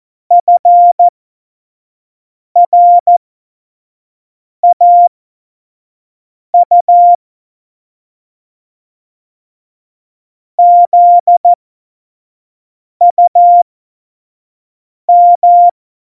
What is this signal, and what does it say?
Morse code, heard as a keyed tone, about 14 wpm, spaced as for 5 wpm, 700 Hz. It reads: FRAU ZUM
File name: Morsekurs / Witze / Witz 16. Januar 2018, 25 F70